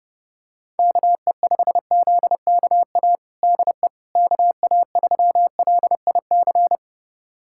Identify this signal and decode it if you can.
Morse code, heard as a keyed tone, about 32 wpm, 700 Hz. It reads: KE5ZKA DE KA3LIC
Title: USA Callsign DE Callsign 30wpm